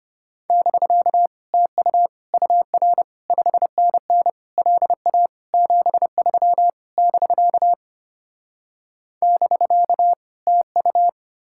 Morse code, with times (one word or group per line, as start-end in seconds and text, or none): 0.5-1.3 <BK>
1.5-2.1 TU
2.3-3.0 UR
3.3-4.3 5NN
4.6-5.3 LA
5.5-6.7 73
7.0-7.7 <BK>
9.2-10.1 <BK>
10.5-11.1 TU